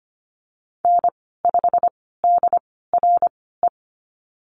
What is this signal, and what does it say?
Morse code, heard as a keyed tone, about 25 wpm, spaced as for 16 wpm, 700 Hz. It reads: N5DRE
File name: Lesson 15g - Number 5 - Call Signs 25wpm